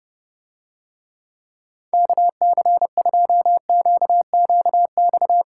Morse code, heard as a keyed tone, about 30 wpm, 700 Hz. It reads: KC2QQX